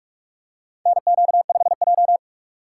Morse code, heard as a keyed tone, about 46 wpm, 700 Hz. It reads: NQ5J
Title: USA Call Signs - 4 Characters - 3x Character Spacing - 45wpm